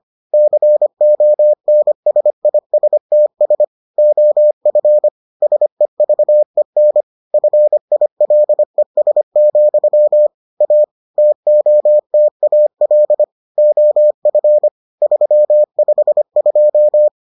Morse code, heard as a keyed tone, about 25 wpm, 600 Hz. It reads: CONSISTS OF SEVEN FILES, A TOTAL OF 352